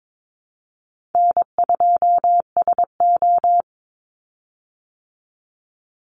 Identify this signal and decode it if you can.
Morse code, heard as a keyed tone, about 22 wpm, 700 Hz. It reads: N2SO
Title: Easy Call Signs Repeated 2x - 22wpm